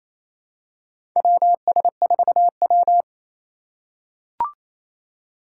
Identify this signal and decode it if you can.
Morse code, heard as a keyed tone, about 28 wpm, 700 Hz. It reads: WS4W
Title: USA Call Signs - 4 Characters - 2x Character Spacing - 28wpm